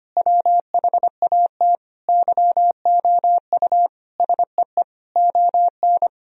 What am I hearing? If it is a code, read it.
Morse code, heard as a keyed tone, about 25 wpm, 700 Hz. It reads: WHAT YOU SEE ON
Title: Sentences from Top 1000 Words - Review Part 4 - 25wpm